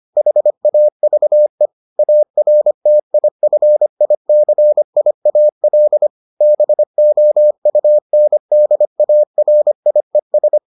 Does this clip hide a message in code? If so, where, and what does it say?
Yes: Morse code from 0.2 to 10.6 s, heard as a keyed tone, about 25 wpm, 600 Hz. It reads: HAVE ARTIFICIAL BOUNDARIES